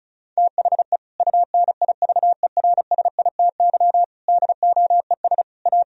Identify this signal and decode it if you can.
Morse code, heard as a keyed tone, about 36 wpm, 700 Hz. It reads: THE UNIVERSITY DOES A
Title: Sentences from Top 1000 Words - Review Part 4 - 35wpm